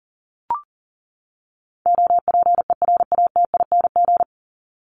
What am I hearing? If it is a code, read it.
Morse code, heard as a keyed tone, about 39 wpm, 700 Hz. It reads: OPERATING
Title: ING Suffix - Top 100 Words with Base Words 40wpm